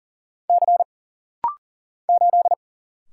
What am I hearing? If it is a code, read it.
Morse code, heard as a keyed tone, about 40 wpm, 700 Hz. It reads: C 8
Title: Single Letter-Number - Rapid-Fire 40wpm